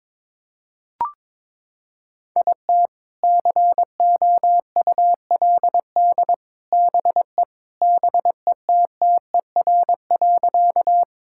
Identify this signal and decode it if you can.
Morse code, heard as a keyed tone, about 22 wpm, 700 Hz. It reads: IT COULD BE BETTER.
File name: practice / 22wpm / Sentences with 2-Word Phrases - 22wpm